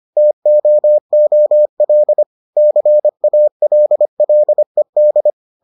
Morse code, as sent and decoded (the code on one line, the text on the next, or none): - --- --- .-.. / -.-. .- .-.. .-.. -..
TOOL CALLED